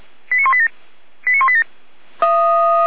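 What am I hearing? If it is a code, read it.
Five-tone selective call (ZVEI), heard as quick runs of five tones, tones of about 70 ms. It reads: 89187 89187
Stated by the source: Five-tone ZVEI selective call received on an SDR.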